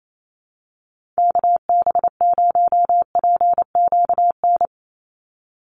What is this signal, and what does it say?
Morse code, heard as a keyed tone, about 28 wpm, 700 Hz. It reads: KB0PQN
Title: USA Call Signs 28wpm